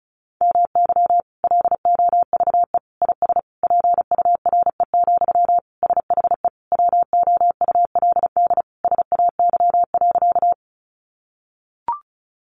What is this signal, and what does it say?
Morse code, heard as a keyed tone, about 34 wpm, 700 Hz. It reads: MY LOVE IS PURE, SHE WOULD SAY.
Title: Sentences from Top 1500 Words 35wpm